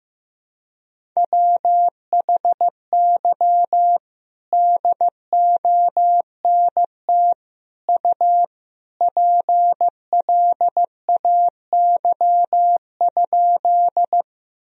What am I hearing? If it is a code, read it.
Morse code, heard as a keyed tone, about 15 wpm, 700 Hz. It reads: WHY DONT U PLAY?
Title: Sentences from Top 300 Words and Abbreviations 15wpm